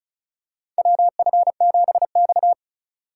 Morse code, heard as a keyed tone, about 36 wpm, 700 Hz. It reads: WF7X
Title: USA Call Signs - 4 Characters - 4x Character Spacing - 35wpm